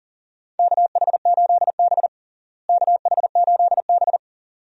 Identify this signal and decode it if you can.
Morse code, heard as a keyed tone, about 40 wpm, 700 Hz. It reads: KH8B KH8B
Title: USA Call Signs - 4 Characters - Repeated 2x - 40wpm